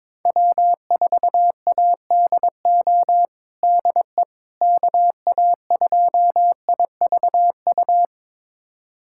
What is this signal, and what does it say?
Morse code, heard as a keyed tone, about 22 wpm, 700 Hz. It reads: W4ADO DE KA2IVU
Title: USA Callsign DE Callsign 22wpm